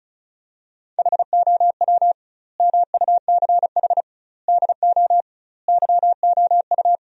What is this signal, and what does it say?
Morse code, heard as a keyed tone, about 34 wpm, 700 Hz. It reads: HOW MUCH DO YOU
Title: 2x Word-Spacing - Sentences from Top 200 Words - 35wpm